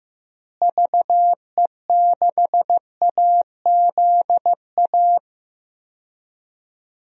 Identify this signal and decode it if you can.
Morse code, heard as a keyed tone, about 15 wpm, 700 Hz. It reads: VE6AZA
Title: DXCC-100 Call Signs - 6 Characters - Repeated 3x - 15wpm